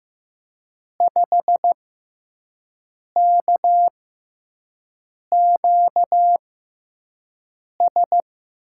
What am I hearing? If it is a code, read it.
Morse code, heard as a keyed tone, about 15 wpm, spaced as for 5 wpm, 700 Hz. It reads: 5KQS
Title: USA Call Signs - 6x Character Spacing - 15wpm